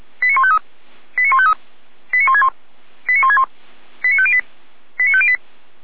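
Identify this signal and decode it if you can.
Five-tone selective call (ZVEI), heard as quick runs of five tones, tones of about 70 ms. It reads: 89153 89153 89161 89161 89508 89508
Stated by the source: Five-tone ZVEI selective call received by SDR, from Feuerwehr Orlamünde.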